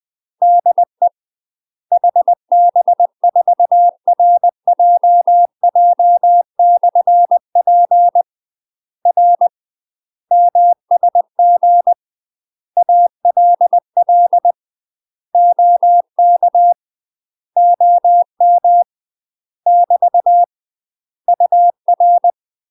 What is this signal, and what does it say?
Morse code, heard as a keyed tone, about 20 wpm, 700 Hz. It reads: DE HB4RJJ/P R MSG ALL OK OM <BT> UR
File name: UK MASTER QSOs TRAINING / 20WPM / UK MASTER QSOs 0047